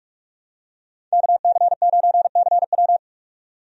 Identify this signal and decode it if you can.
Morse code, heard as a keyed tone, about 45 wpm, 700 Hz. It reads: KC9CW
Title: USA Call Signs - 5 Characters - 45wpm